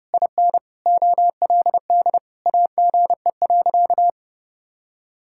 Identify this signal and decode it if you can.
Morse code, heard as a keyed tone, about 32 wpm, 700 Hz. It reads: IN OLD AGE.
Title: Sentences from Top 1000 Words - Review Part 6 - 30wpm